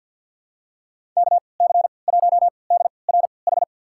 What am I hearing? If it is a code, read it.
Morse code, heard as a keyed tone, about 47 wpm, spaced as for 28 wpm, 700 Hz. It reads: KC1DRH
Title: USA Call Signs - 6 Characters - 3x Character Spacing - 50wpm